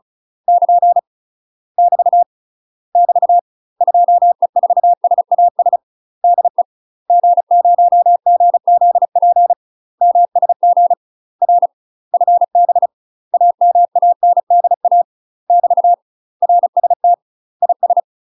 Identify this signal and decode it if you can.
Morse code, heard as a keyed tone, about 36 wpm, 700 Hz. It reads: <KN> <BT> <BT> 2E4SAS DE G0GZP MSG R FB AMANDA <BT> RST IS
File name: UK MASTER QSOs TRAINING / 35WPM / UK MASTER QSOs 0032